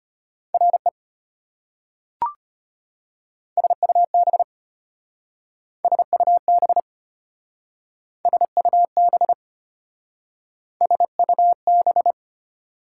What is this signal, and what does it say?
Morse code, heard as a keyed tone, about 33 wpm, 690 Hz. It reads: RE SUB SUB SUB SUB